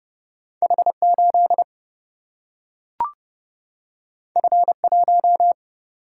Morse code, heard as a keyed tone, about 30 wpm, 700 Hz. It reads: H8 F1
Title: Single-Letter Single-Number 30wpm